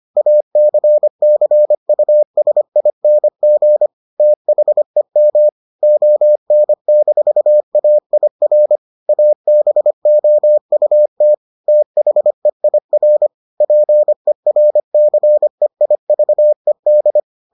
Morse code, heard as a keyed tone, about 25 wpm, 600 Hz. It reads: ACCUSING THEM ON-AIR ABOUT THEIR PERCEIVED